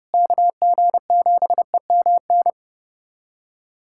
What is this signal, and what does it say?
Morse code, heard as a keyed tone, about 32 wpm, 700 Hz. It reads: KG7EMN
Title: USA Callsign DE Callsign - 6 Characters - 30wpm